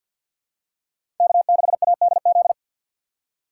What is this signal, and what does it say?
Morse code, heard as a keyed tone, about 50 wpm, 700 Hz. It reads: K6ADZ
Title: USA Call Signs - 5 Characters - 2x Character Spacing - 50wpm